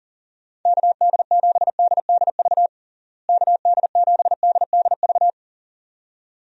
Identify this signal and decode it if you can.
Morse code, heard as a keyed tone, about 37 wpm, 700 Hz. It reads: KD7DDV KD7DDV
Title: USA Call Signs - 6 Characters - Repeated 2x - 40wpm